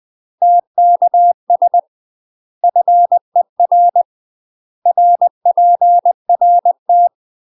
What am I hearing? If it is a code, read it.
Morse code, heard as a keyed tone, about 20 wpm, 700 Hz. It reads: TKS FER RPRT
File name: UK MASTER QSOs TRAINING / 20WPM / UK MASTER QSOs 0069